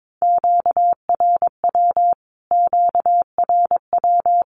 Morse code, heard as a keyed tone, about 22 wpm, 700 Hz. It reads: QRW QRW
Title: Q-Codes Repeated 3x 22wpm